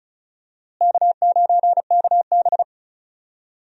Morse code, heard as a keyed tone, about 34 wpm, 700 Hz. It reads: K9KB
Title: USA Call Signs - 4 Characters - Repeated 3x - 35wpm